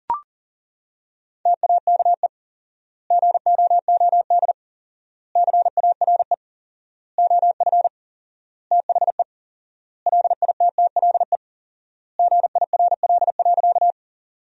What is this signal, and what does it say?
Morse code, heard as a keyed tone, about 40 wpm, 700 Hz. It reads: TAKE GOOD CARE OF THE LITTLE GIRL.
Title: 4x Word-Spacing - Sentences from Top 400 Words - 40wpm